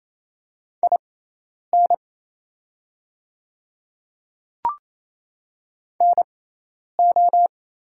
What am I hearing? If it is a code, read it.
Morse code, heard as a keyed tone, about 28 wpm, spaced as for 10 wpm, 700 Hz. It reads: IN NO
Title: 6x Character Spacing - Top 100 Words - 28wpm